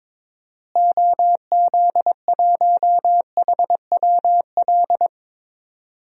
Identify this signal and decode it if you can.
Morse code, heard as a keyed tone, about 22 wpm, 700 Hz. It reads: OZ1HWL
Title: DXCC-100 Call Signs - 6 Characters - Repeated 3x - 22wpm